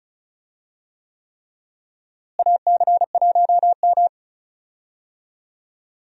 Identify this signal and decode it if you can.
Morse code, heard as a keyed tone, about 36 wpm, 700 Hz. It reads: AC1M
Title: USA Call Signs - 4 Characters - Repeated 3x - 35wpm